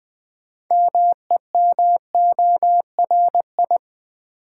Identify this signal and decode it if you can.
Morse code, heard as a keyed tone, about 20 wpm, 700 Hz. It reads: MEMORI
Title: Build A Word - 3 Letters At A Time - 8-Letter Words - 20wpm